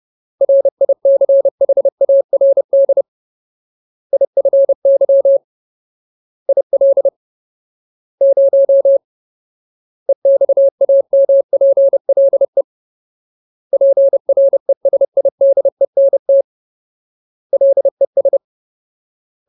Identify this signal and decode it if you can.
Morse code, heard as a keyed tone, about 30 wpm, 550 Hz. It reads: RICHARD IFY IL 0 EXAMPLE PRESIDENT LES
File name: Ultimate Word List / 30WPM Ultimate